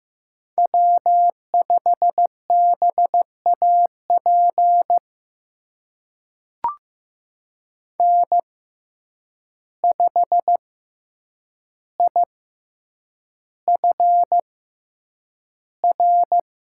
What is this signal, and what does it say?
Morse code, heard as a keyed tone, about 15 wpm, spaced as for 8 wpm, 700 Hz. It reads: W5BAP N5IFR
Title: USA Call Signs - 5 Characters - 6x Character Spacing - 15wpm